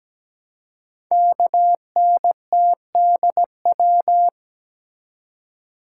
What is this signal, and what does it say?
Morse code, heard as a keyed tone, about 17 wpm, 700 Hz. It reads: KNTDW
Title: Code Groups - 5 Letters - 17wpm